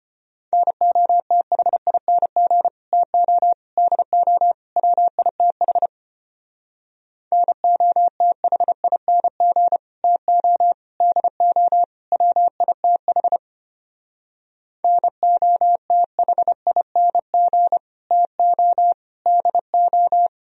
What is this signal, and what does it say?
Morse code, heard as a keyed tone, about 32 wpm, 700 Hz. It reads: NOTHING TO DO WITH NOTHING TO DO WITH NOTHING TO DO